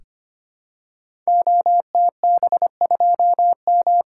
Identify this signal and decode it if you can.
Morse code, heard as a keyed tone, about 25 wpm, 700 Hz. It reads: OTB2M
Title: Code Groups - 5 Letter:Numbers - 25wpm